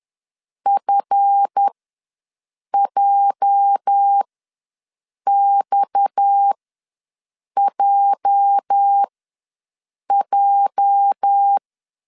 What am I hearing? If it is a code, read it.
Morse code, heard as a keyed tone, about 11 wpm, spaced as for 6 wpm, 800 Hz. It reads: FJXJJ
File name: LEZIONI / 8 YJ45 / 8 YJ ws 10wpm 4043